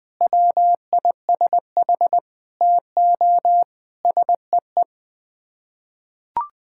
Morse code, heard as a keyed tone, about 20 wpm, 700 Hz. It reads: WISH TO SEE
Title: Sets of 9 Words - 20wpm